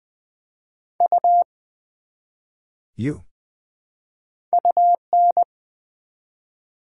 Morse code, heard as a keyed tone, about 20 wpm, 700 Hz. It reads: U UN
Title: Build A Word - 1 Letter At A Time - 3 to 5-Letter Words - 20wpm